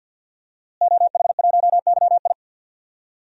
Morse code, heard as a keyed tone, about 50 wpm, 700 Hz. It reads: OH1YI
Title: DXCC-100 Call Signs - 5 Characters - Repeated 2x - 50wpm